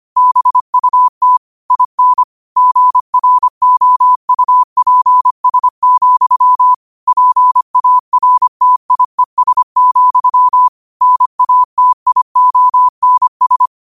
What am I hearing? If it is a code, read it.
Morse code, heard as a keyed tone, about 25 wpm, 1000 Hz. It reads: BUT IN GROUPS, PARTIES, NATIONS
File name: qotdfiles / 25wpm